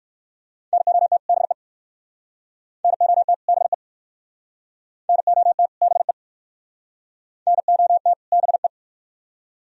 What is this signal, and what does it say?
Morse code, heard as a keyed tone, about 61 wpm, 700 Hz. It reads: NOT BE NOT BE NOT BE NOT BE